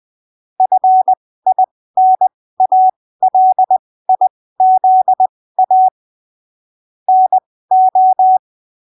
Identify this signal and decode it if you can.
Morse code, heard as a keyed tone, about 20 wpm, spaced as for 15 wpm, 750 Hz. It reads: FINALIZA NO
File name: arquivos / texto / texto 27 20ppm